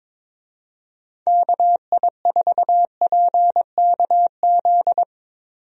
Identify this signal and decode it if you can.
Morse code, heard as a keyed tone, about 21 wpm, 700 Hz. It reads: KI4PKZ
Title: USA Call Signs - 6 Characters - Repeated 3x - 22wpm